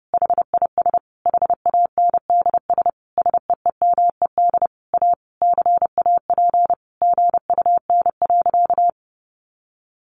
Morse code, heard as a keyed tone, about 30 wpm, 700 Hz. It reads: HIS HANDS SEEMED A CAP GUN.
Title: A Sound Of Thunder 30wpm